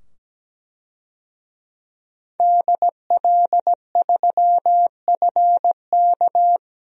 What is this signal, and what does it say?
Morse code, heard as a keyed tone, about 17 wpm, 700 Hz. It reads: DL3FK
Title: DXCC-100 Call Signs - 5 Characters - 17wpm